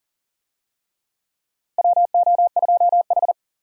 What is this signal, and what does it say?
Morse code, heard as a keyed tone, about 40 wpm, 700 Hz. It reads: WO2H
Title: USA Call Signs - 4 Characters - Repeated 3x - 40wpm